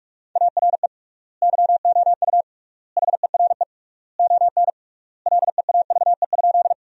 Morse code, heard as a keyed tone, about 42 wpm, 700 Hz. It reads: ARE YOU HERE ON LEAVE?
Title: 3x Word-Spacing - Sentences from Top 400 Words - 45wpm